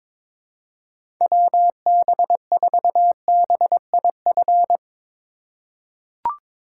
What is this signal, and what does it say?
Morse code, heard as a keyed tone, about 22 wpm, 700 Hz. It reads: WB4BIF
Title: USA Call Signs - 8x Character Spacing - 22wpm